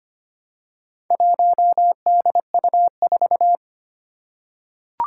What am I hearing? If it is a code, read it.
Morse code, heard as a keyed tone, about 25 wpm, 700 Hz. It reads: 1DU4
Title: Four Challenging Letter-Number Combinations 25wpm